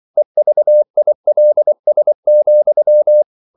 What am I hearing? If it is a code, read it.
Morse code, heard as a keyed tone, about 24 wpm, 600 Hz. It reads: EVILS,
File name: CW / Practice / EnglishQuotations0324